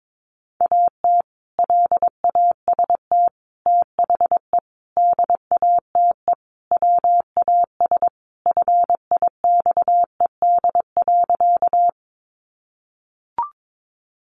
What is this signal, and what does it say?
Morse code, heard as a keyed tone, about 22 wpm, 700 Hz. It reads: AT LAST THE DATE WAS FIXED.
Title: Sentences from Top 1100 Words 22wpm